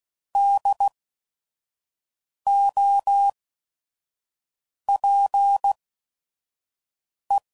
Morse code, heard as a keyed tone, about 16 wpm, spaced as for 5 wpm, 800 Hz. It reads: DOPE
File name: Prefixes / By First Letter / Ds